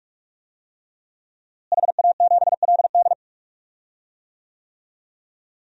Morse code, heard as a keyed tone, about 41 wpm, 700 Hz. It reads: HA7LD
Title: DXCC-100 Call Signs - 5 Characters - Repeated 3x - 45wpm